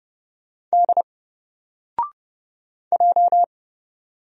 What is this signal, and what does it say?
Morse code, heard as a keyed tone, about 30 wpm, 700 Hz. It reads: D J
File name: Single-Letter-Number-30wpm